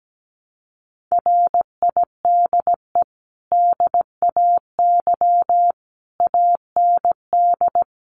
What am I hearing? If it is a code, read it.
Morse code, heard as a keyed tone, about 17 wpm, 700 Hz. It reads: RIDE DAY AND
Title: Sentences from Top 1200 Words 17wpm